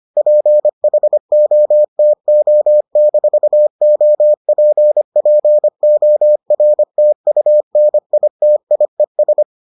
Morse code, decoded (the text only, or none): PHOTO-OPPORTUNITIES